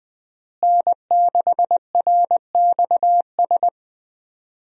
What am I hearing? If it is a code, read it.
Morse code, heard as a keyed tone, about 20 wpm, 700 Hz. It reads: N6RXS
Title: USA Call Signs - 5 Characters - Repeated 3x - 20wpm